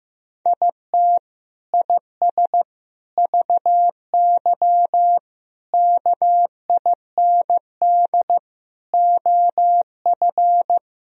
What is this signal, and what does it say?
Morse code, heard as a keyed tone, about 15 wpm, 700 Hz. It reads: IT IS VY KIND OF
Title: Sentences from Top 200 Words and Abbreviations 15wpm